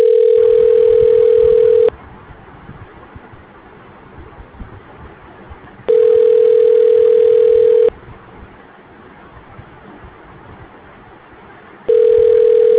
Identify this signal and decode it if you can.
Ringback tone (North America), 440 + 480 Hz.